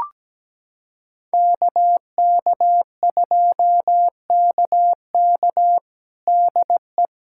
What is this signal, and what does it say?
Morse code, heard as a keyed tone, about 17 wpm, 700 Hz. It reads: KK2KK DE
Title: USA Callsign DE Callsign - 5 Characters - 17wpm